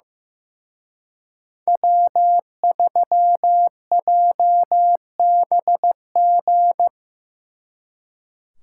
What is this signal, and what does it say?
Morse code, heard as a keyed tone, about 18 wpm, 700 Hz. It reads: W3JBG
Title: Speed Racing - USA 1x3 Call Signs 15wpm